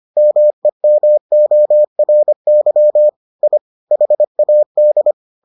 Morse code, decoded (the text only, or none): MEMORY I HAD